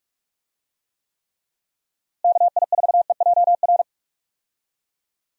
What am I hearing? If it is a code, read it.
Morse code, heard as a keyed tone, about 42 wpm, 700 Hz. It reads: KI4EJR
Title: USA Call Signs - 6 Characters - Repeated 3x - 45wpm